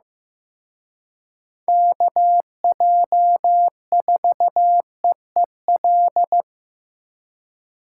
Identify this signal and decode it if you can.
Morse code, heard as a keyed tone, about 18 wpm, 700 Hz. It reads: KJ4EEL
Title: Speed Racing - USA 2x3 Call Signs 15wpm